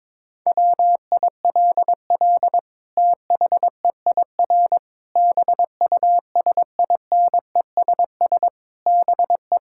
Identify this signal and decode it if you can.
Morse code, heard as a keyed tone, about 22 wpm, 700 Hz. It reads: WILL THEIR BUSINESS BE
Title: Sentences from Top 1000 Words - Review Part 5 - 22wpm